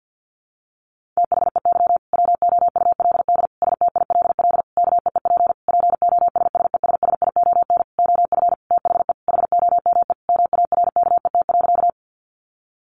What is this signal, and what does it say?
Morse code, heard as a keyed tone, about 50 wpm, 700 Hz. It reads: THEY WOULD STILL KEEP POSSESSION OF THE HOME MARKET.
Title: Sentences from Top 1400 Words 50wpm